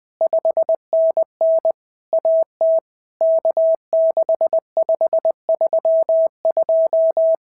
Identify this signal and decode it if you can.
Morse code, heard as a keyed tone, about 19 wpm, 650 Hz. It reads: 5NN AT K6532
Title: POTA QSO Practice - 20wpm